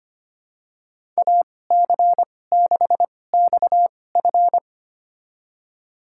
Morse code, heard as a keyed tone, about 25 wpm, spaced as for 18 wpm, 700 Hz. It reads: AC6XF
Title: USA Call Signs - 5 Characters - 2x Character Spacing - 25wpm